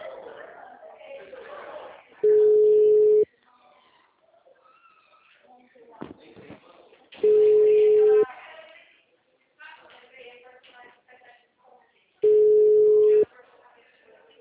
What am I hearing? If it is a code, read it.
Ringback tone (Europe, 425 Hz), with speech.
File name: out- -2007- - - .4